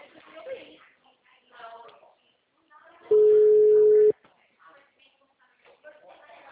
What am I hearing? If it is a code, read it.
Ringback tone (Europe, 425 Hz), with speech.